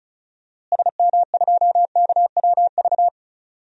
Morse code, heard as a keyed tone, about 34 wpm, 700 Hz. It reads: SM2KWV
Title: DXCC-100 Call Signs - 6 Characters - Repeated 2x - 35wpm